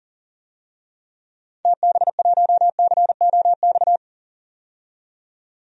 Morse code, heard as a keyed tone, about 39 wpm, 700 Hz. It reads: TB1COX
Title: DXCC-100 Call Signs - 6 Characters - 40wpm